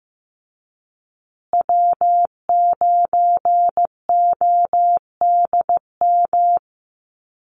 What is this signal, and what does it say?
Morse code, heard as a keyed tone, about 15 wpm, 700 Hz. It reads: W9ODM
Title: USA Call Signs Repeated 3x 15wpm